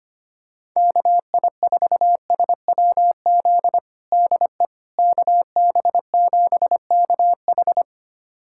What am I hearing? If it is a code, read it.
Morse code, heard as a keyed tone, about 25 wpm, 700 Hz. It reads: KI4SWZ DE KB7KH
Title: USA Callsign DE Callsign 25wpm